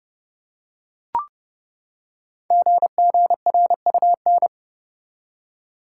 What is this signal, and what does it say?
Morse code, heard as a keyed tone, about 30 wpm, 700 Hz. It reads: GGRUN
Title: Code Groups - 5 Letters - 30wpm